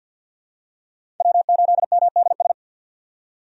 Morse code, heard as a keyed tone, about 51 wpm, 700 Hz. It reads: W8MDS
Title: USA Call Signs - 5 Characters - 50wpm